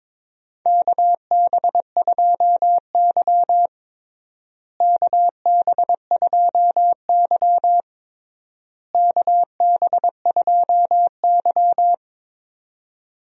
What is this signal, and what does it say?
Morse code, heard as a keyed tone, about 22 wpm, 700 Hz. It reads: KB2Y KB2Y KB2Y
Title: USA Call Signs - 4 Characters - Repeated 3x - 22wpm